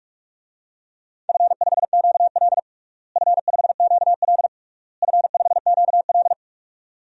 Morse code, heard as a keyed tone, about 42 wpm, 700 Hz. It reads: F5QL F5QL F5QL
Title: DXCC-100 Call Signs - 4 Characters - Repeated 3x - 45wpm